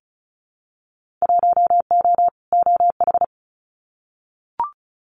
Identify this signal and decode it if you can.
Morse code, heard as a keyed tone, about 36 wpm, 690 Hz. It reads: 1O OH
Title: Winter Field Day Exchanges 35wpm